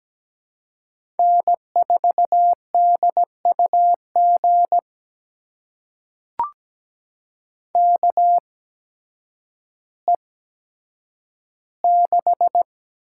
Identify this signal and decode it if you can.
Morse code, heard as a keyed tone, about 17 wpm, spaced as for 9 wpm, 700 Hz. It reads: N4DUG KE6
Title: USA Call Signs - 5 Characters - 8x Character Spacing - 17wpm